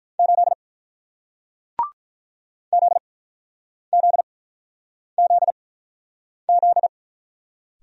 Morse code, heard as a keyed tone, about 47 wpm, spaced as for 9 wpm, 700 Hz. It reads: 8 ZZZZ